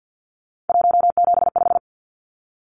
Morse code, heard as a keyed tone, about 50 wpm, 700 Hz. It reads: 175
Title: Numbers - 3-Digit 50wpm